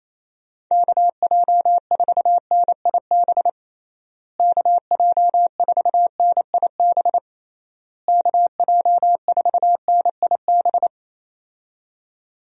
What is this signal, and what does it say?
Morse code, heard as a keyed tone, about 28 wpm, 700 Hz. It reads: KJ4NIB KJ4NIB KJ4NIB
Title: USA Call Signs - 6 Characters - Repeated 3x - 28wpm